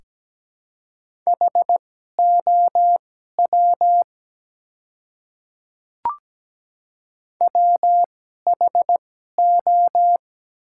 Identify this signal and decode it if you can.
Morse code, heard as a keyed tone, about 17 wpm, spaced as for 12 wpm, 700 Hz. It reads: HOW WHO
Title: 2x Character Spacing - Top 100 Words - 17wpm